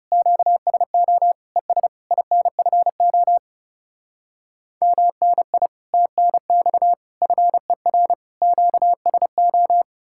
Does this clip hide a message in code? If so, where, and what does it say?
Yes: Morse code from 0.1 to 9.8 s, heard as a keyed tone, about 39 wpm, 700 Hz. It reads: QSO ES INFO MNI TNX FER QSO